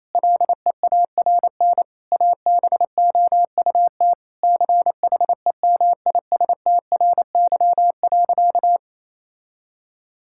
Morse code, heard as a keyed tone, about 28 wpm, 700 Hz. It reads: LEARN ABOUT CHEMISTRY.
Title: Sentences from Top 1000 Words - Review Part 6 - 28wpm